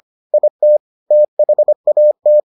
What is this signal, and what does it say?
Morse code, heard as a keyed tone, about 25 wpm, 600 Hz. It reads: IT THAT